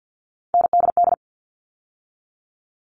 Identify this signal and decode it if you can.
Morse code, heard as a keyed tone, about 48 wpm, 700 Hz. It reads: NDD
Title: All Cut Numbers - 3-Digit 50wpm